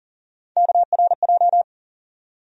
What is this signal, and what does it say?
Morse code, heard as a keyed tone, about 40 wpm, 700 Hz. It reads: KRJ
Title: USA Call Sign Postfixes - 3 Letters 40wpm